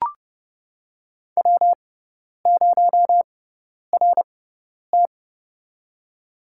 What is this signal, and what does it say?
Morse code, heard as a keyed tone, about 30 wpm, spaced as for 10 wpm, 700 Hz. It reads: W0RT
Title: USA Call Signs - 6x Character Spacing - 30wpm